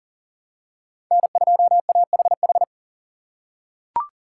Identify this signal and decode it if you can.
Morse code, heard as a keyed tone, about 40 wpm, 690 Hz. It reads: N2AHH
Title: USA Call Signs - 2x Character Spacing - 40wpm